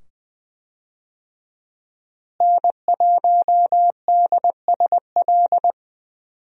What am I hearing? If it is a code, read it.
Morse code, heard as a keyed tone, about 20 wpm, 700 Hz. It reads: N1DSL
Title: USA Call Signs - 5 Characters - Repeated 2x - 20wpm